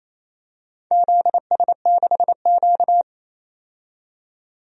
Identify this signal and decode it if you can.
Morse code, heard as a keyed tone, about 28 wpm, 690 Hz. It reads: ZS6Q
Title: DXCC-100 Call Signs - 4 Characters - 28wpm